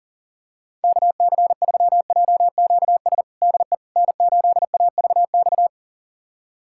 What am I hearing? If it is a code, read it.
Morse code, heard as a keyed tone, about 37 wpm, 700 Hz. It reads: KC3JQS DE N8AVX
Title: USA Callsign DE Callsign 40wpm